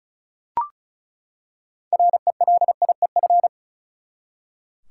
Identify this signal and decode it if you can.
Morse code, heard as a keyed tone, about 36 wpm, 700 Hz. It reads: RELIEF
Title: Six Letter Words - 35wpm